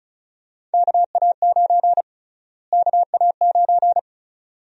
Morse code, heard as a keyed tone, about 36 wpm, 700 Hz. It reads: KA9 KA9
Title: USA Call Sign Prefixes - Repeated 2x - 35wpm